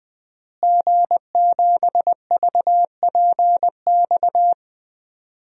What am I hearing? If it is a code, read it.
Morse code, heard as a keyed tone, about 20 wpm, 700 Hz. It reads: G7VPX
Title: DXCC-100 Call Signs - 5 Characters - Repeated 3x - 20wpm